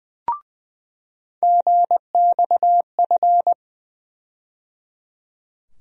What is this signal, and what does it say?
Morse code, heard as a keyed tone, about 20 wpm, 700 Hz. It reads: GXF
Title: USA Call Sign Postfixes - 3 Letters 20wpm